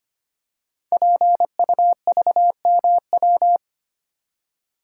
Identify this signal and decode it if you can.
Morse code, heard as a keyed tone, about 25 wpm, 700 Hz. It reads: PUVMW
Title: Code Groups - 5 Letters - 25wpm